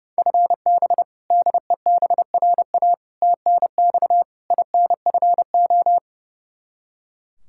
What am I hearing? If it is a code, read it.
Morse code, heard as a keyed tone, about 30 wpm, 700 Hz. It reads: FB DEBRA TNX INFO
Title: Practice QSO - Exchange 3 - More Info - 30wpm